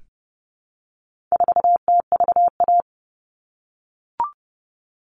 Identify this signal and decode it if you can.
Morse code, heard as a keyed tone, about 30 wpm, 700 Hz. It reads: 4TVA
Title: All Cut Numbers - 4-Digit 30wpm